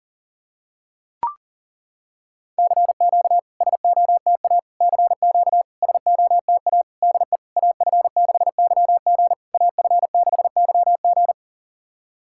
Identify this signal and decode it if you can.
Morse code, heard as a keyed tone, about 40 wpm, 690 Hz. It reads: CQ SOTA CQ SOTA DE AF6YG AF6YG
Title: SOTA QSO Practice 40wpm